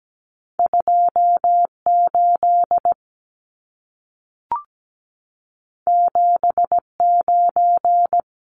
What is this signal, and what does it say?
Morse code, heard as a keyed tone, about 17 wpm, 700 Hz. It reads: 28 79
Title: Numbers - 2-Digit 17wpm